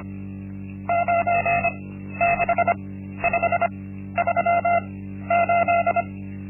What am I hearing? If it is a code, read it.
Morse code, heard as a keyed tone, about 25 wpm, spaced as for 13 wpm, 690 Hz. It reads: 96538
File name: Linksmakalnis-1